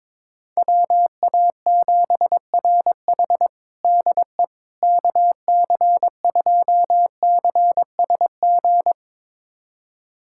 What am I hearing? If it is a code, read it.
Morse code, heard as a keyed tone, about 22 wpm, 700 Hz. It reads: WA7RH DE KC2CSG
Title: USA Callsign DE Callsign 22wpm